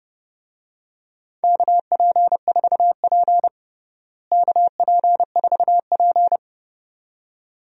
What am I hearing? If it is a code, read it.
Morse code, heard as a keyed tone, about 30 wpm, 700 Hz. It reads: KP4P KP4P
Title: DXCC-100 Call Signs - 4 Characters - Repeated 2x - 30wpm